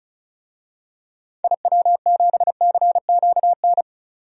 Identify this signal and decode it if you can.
Morse code, heard as a keyed tone, about 36 wpm, 700 Hz. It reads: IW7CQN
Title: DXCC-100 Call Signs - 6 Characters - 35wpm